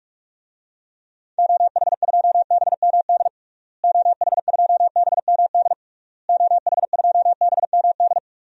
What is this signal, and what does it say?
Morse code, heard as a keyed tone, about 41 wpm, 700 Hz. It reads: OH2BMD OH2BMD OH2BMD
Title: DXCC-100 Call Signs - 6 Characters - Repeated 3x - 45wpm